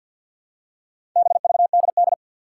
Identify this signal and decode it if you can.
Morse code, heard as a keyed tone, about 45 wpm, 700 Hz. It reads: BVDD
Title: Four Challenging Letter-Number Combinations 50wpm